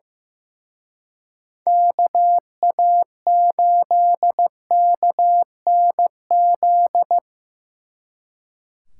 Morse code, heard as a keyed tone, about 18 wpm, 700 Hz. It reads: KA8KNZ
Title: Speed Racing - USA 2x3 Call Signs 15wpm